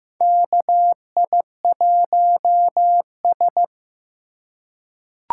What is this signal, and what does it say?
Morse code, heard as a keyed tone, about 15 wpm, 700 Hz. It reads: KI1S
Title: USA Call Signs - 4 Characters - 2x Character Spacing - 15wpm